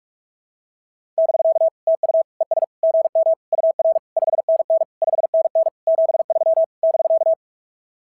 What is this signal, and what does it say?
Morse code, heard as a keyed tone, about 44 wpm, 690 Hz. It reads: <BK> TU ES GM UR 5NN 5NN 73 <BK>
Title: SOTA QSO Practice 45wpm